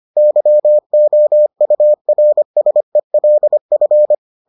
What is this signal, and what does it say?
Morse code, heard as a keyed tone, about 25 wpm, 600 Hz. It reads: YOURSELF